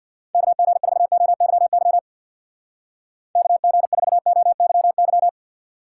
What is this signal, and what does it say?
Morse code, heard as a keyed tone, about 65 wpm, 700 Hz. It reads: KG4OYY KG4OYY